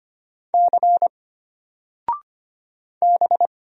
Morse code, heard as a keyed tone, about 25 wpm, 700 Hz. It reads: C B
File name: Single-Letter-Number-25wpm